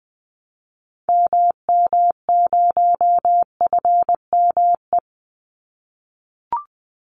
Morse code, heard as a keyed tone, about 20 wpm, 700 Hz. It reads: MM0FME
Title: Call Signs - 20wpm